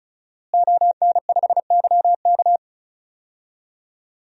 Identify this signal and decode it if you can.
Morse code, heard as a keyed tone, about 36 wpm, 700 Hz. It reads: ON5YK
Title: DXCC-100 Call Signs - 5 Characters - Repeated 2x - 35wpm